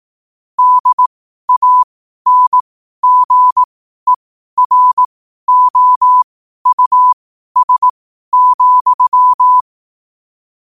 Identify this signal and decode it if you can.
Morse code, heard as a keyed tone, about 18 wpm, spaced as for 12 wpm, 1000 Hz. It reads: DANGEROUS,
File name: qotdfiles / 12wpm